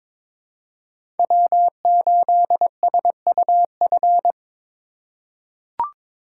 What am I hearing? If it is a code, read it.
Morse code, heard as a keyed tone, about 22 wpm, 700 Hz. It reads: W8SUF
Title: USA Call Signs - 5 Characters - 2x Character Spacing - 22wpm